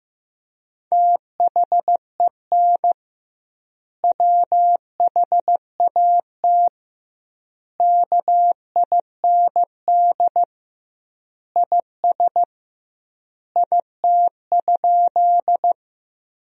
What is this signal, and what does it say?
Morse code, heard as a keyed tone, about 15 wpm, 700 Hz. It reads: THEN WHAT KIND IS IT?
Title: 2x Word-Spacing - Sentences from Top 200 Words - 15wpm